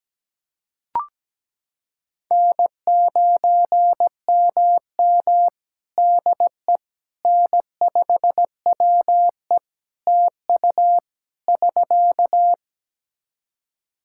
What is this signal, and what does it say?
Morse code, heard as a keyed tone, about 17 wpm, 700 Hz. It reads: N9MM DE N5WE TU <SK>
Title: Practice QSO - Exchange 4 - Ending - 17wpm